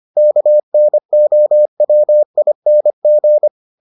Morse code, heard as a keyed tone, about 25 wpm, 600 Hz. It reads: KNOWING